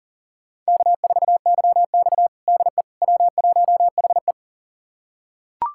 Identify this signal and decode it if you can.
Morse code, heard as a keyed tone, about 37 wpm, 700 Hz. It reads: K4YX DE W1HE
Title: USA Callsign DE Callsign - 4 Characters - 40wpm